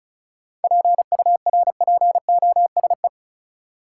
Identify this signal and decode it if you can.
Morse code, heard as a keyed tone, about 34 wpm, 700 Hz. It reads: PURPOSE
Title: Sentences from Top 800 Words - New Word Drill - 35wpm